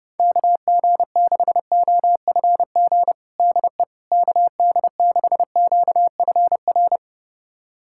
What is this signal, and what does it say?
Morse code, heard as a keyed tone, about 30 wpm, 700 Hz. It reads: KG6OFG DE KD6QFR
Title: USA Callsign DE Callsign 30wpm